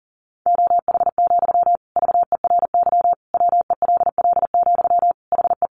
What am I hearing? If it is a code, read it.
Morse code, heard as a keyed tone, about 40 wpm, 700 Hz. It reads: OH, VERY WELL, HE
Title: Sentences from Top 500 Words 40wpm